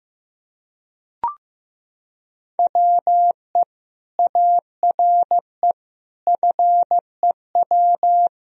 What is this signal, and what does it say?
Morse code, heard as a keyed tone, about 15 wpm, 700 Hz. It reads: WE ARE FEW
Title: Sentences from Top 1000 Words - Review Part 1 - 15wpm